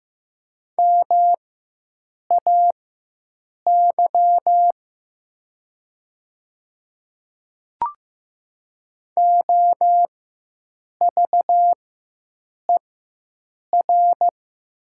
Morse code, heard as a keyed tone, about 15 wpm, spaced as for 7 wpm, 700 Hz. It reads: MAY OVER
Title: 4x Character Spacing - Top 100 Words - 15wpm